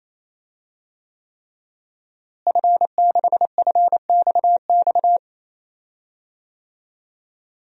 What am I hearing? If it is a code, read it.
Morse code, heard as a keyed tone, about 28 wpm, 700 Hz. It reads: F6FXX
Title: DXCC-100 Call Signs - 5 Characters - Repeated 3x - 28wpm